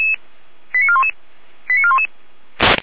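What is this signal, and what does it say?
Five-tone selective call (ZVEI), heard as quick runs of five tones, tones of about 70 ms. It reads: 89411 89411
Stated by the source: ZVEI five-tone call received on an SDR.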